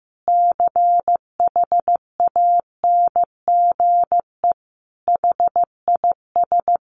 Morse code, heard as a keyed tone, about 15 wpm, 700 Hz. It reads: CHANGE HIS
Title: Sentences from Top 1200 Words 15wpm